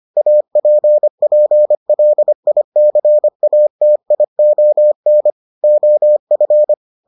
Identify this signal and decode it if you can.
Morse code, heard as a keyed tone, about 25 wpm, 600 Hz. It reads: APPLICATION OF